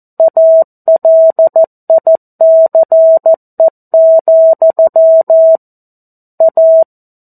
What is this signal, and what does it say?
Morse code, heard as a keyed tone, about 14 wpm, 650 Hz. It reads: ALICE, A
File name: 02-Chapter-02 14 WPM